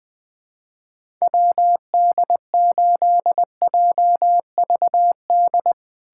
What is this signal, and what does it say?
Morse code, heard as a keyed tone, about 21 wpm, 700 Hz. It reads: WD8JVD
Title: USA Callsign DE Callsign 20wpm